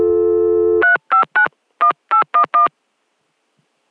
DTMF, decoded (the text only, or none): 3591811